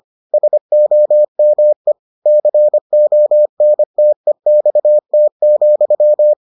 Morse code, heard as a keyed tone, about 25 wpm, 600 Hz. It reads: SOME CONTEXT,